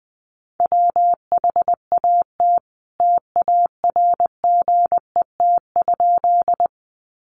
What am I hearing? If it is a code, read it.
Morse code, heard as a keyed tone, about 20 wpm, 700 Hz. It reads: WHAT TARGET?
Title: Sentences from Top 1200 Words 20wpm